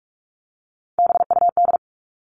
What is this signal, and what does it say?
Morse code, heard as a keyed tone, about 44 wpm, 700 Hz. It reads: BUD
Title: All Cut Numbers - 3-Digit 45wpm